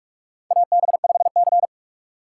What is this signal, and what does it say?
Morse code, heard as a keyed tone, about 43 wpm, 700 Hz. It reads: AB5C
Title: K1USN SST QSO Practice - 45wpm